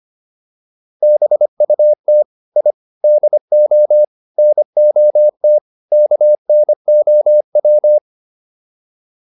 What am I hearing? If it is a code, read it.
Morse code, heard as a keyed tone, about 25 wpm, 600 Hz. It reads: BUT I DO NOT KNOW